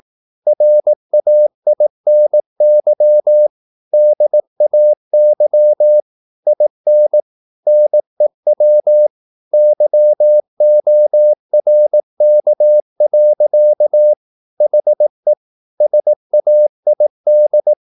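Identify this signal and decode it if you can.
Morse code, heard as a keyed tone, about 18 wpm, 600 Hz. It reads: RAINY DAY IN NEW YORK. HE SAID
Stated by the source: Morse practice at 18 wpm, English text, from CW med Gnister.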